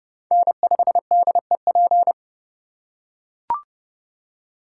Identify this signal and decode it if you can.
Morse code, heard as a keyed tone, about 29 wpm, 700 Hz. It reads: N5DEP